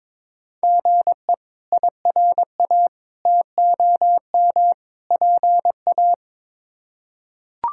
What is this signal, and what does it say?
Morse code, heard as a keyed tone, about 22 wpm, 700 Hz. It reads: GE IRA TOM PA
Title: K1USN SST QSO Practice - 22wpm